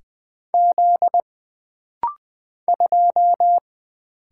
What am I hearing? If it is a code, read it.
Morse code, heard as a keyed tone, about 20 wpm, 700 Hz. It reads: Z 2
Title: Single Letter-Number - Rapid-Fire 20wpm